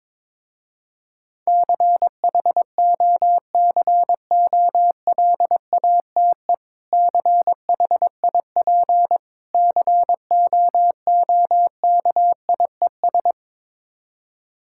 Morse code, heard as a keyed tone, about 22 wpm, 700 Hz. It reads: CHOCOLATE CHIP COOKIES
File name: Difficult Words with Hints - 22wpm